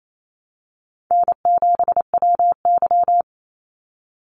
Morse code, heard as a keyed tone, about 28 wpm, 700 Hz. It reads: N7WY
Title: Easy Call Signs Repeated 3x - 28wpm